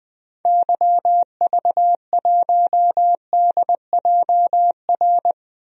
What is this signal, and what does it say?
Morse code, heard as a keyed tone, about 21 wpm, 700 Hz. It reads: YV1DJR
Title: DXCC-100 Call Signs - 6 Characters - 20wpm